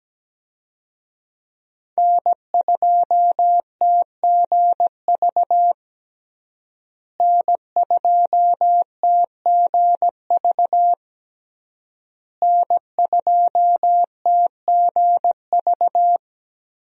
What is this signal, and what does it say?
Morse code, heard as a keyed tone, about 17 wpm, 700 Hz. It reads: N2TGV N2TGV N2TGV
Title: USA Call Signs - 5 Characters - Repeated 3x - 17wpm